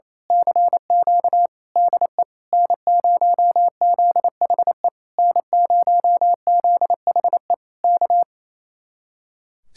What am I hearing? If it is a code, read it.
Morse code, heard as a keyed tone, about 26 wpm, 700 Hz. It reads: CQ DE N0ZHE N0ZHE K
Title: Speed-Racing Common QSO Phrases Encore - 28wpm